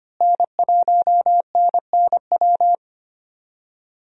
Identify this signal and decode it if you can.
Morse code, heard as a keyed tone, about 25 wpm, 700 Hz. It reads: N1NNW
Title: USA Call Signs - 5 Characters - Repeated 3x - 25wpm